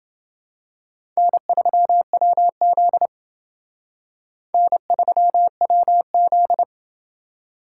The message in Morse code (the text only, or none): N3WZ N3WZ